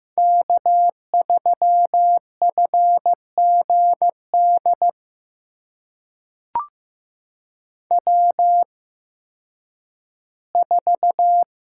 Morse code, heard as a keyed tone, about 15 wpm, spaced as for 10 wpm, 700 Hz. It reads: K3FGD W4
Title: USA Call Signs - 5 Characters - 8x Character Spacing - 15wpm